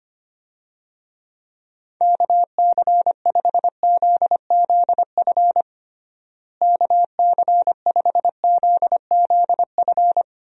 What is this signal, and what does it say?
Morse code, heard as a keyed tone, about 25 wpm, 700 Hz. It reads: KC5ZZF KC5ZZF
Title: USA Call Signs - 6 Characters - Repeated 3x - 25wpm